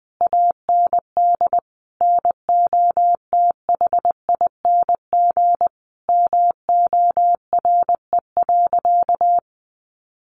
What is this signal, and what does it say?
Morse code, heard as a keyed tone, about 20 wpm, 700 Hz. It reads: AND NOTHING MORE.
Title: Sentences from Top 800 Words 20wpm